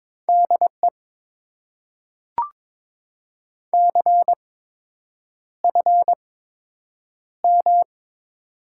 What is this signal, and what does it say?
Morse code, heard as a keyed tone, about 22 wpm, spaced as for 8 wpm, 700 Hz. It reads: DE CFM
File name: Top 90 QSO Elements - 8x Character Spacing - 22wpm